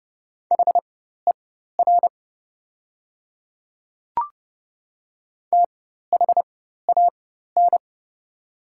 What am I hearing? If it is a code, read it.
Morse code, heard as a keyed tone, about 28 wpm, spaced as for 14 wpm, 700 Hz. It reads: HER THAN
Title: 4x Character Spacing - Top 100 Words - 30wpm